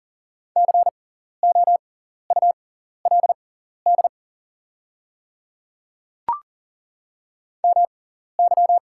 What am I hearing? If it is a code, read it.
Morse code, heard as a keyed tone, about 40 wpm, spaced as for 14 wpm, 690 Hz. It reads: COULD MY
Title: 6x Character Spacing - Top 300 Words - 40wpm